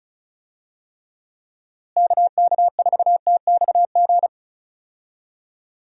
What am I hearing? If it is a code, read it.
Morse code, heard as a keyed tone, about 36 wpm, 700 Hz. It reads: KK4TXG
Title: USA Call Signs - 6 Characters - Repeated 2x - 35wpm